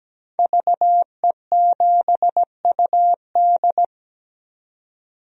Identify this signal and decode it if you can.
Morse code, heard as a keyed tone, about 17 wpm, 700 Hz. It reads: VE7UD
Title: DXCC-100 Call Signs - 5 Characters - 17wpm